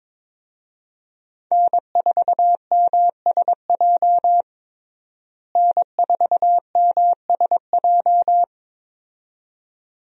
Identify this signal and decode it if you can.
Morse code, heard as a keyed tone, about 22 wpm, 700 Hz. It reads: N4MSJ N4MSJ
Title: USA Call Signs - 5 Characters - Repeated 2x - 22wpm